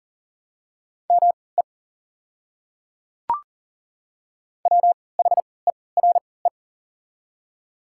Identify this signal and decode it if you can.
Morse code, heard as a keyed tone, about 40 wpm, spaced as for 23 wpm, 700 Hz. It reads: ME WHERE
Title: 3x Character Spacing - Top 500 Words - 40wpm